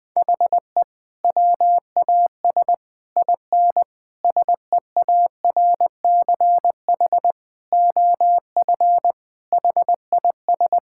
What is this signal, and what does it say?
Morse code, heard as a keyed tone, about 20 wpm, 700 Hz. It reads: HE WAS IN SEARCH OF HIS
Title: Sentences from Top 1000 Words - Review Part 5 - 20wpm